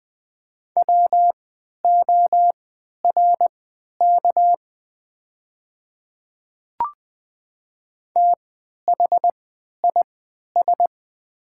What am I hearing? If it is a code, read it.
Morse code, heard as a keyed tone, about 20 wpm, spaced as for 11 wpm, 700 Hz. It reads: WORK THIS
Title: 3x Character Spacing - Top 300 Words - 20wpm